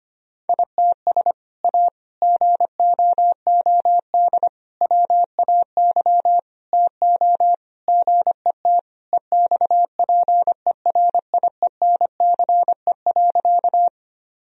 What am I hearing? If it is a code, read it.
Morse code, heard as a keyed tone, about 25 wpm, 700 Hz. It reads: ITS A GOOD WAY TO GET EXPERIENCE.
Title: Sentences from Top 1000 Words - Review Part 3 - 25wpm